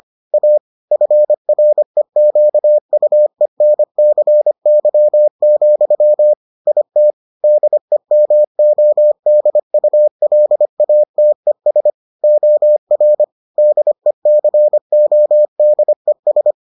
Morse code, heard as a keyed tone, about 25 wpm, 600 Hz. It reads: A FREQUENCY, IT DEMODULATES OR DECODES